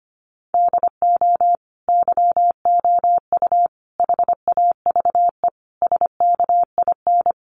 Morse code, heard as a keyed tone, about 25 wpm, 700 Hz. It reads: DO YOU HAVE SKIN
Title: Sentences from Top 1300 Words 25wpm